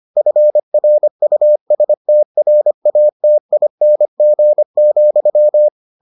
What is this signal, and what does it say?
Morse code, heard as a keyed tone, about 25 wpm, 600 Hz. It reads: FRUSTRATING,